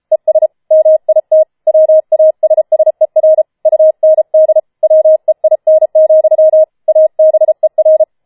Morse code, heard as a keyed tone, about 33 wpm, 630 Hz. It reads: ES MIT WASSER UND WEIN, ABER
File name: Schneewittchen Teil5 34wpm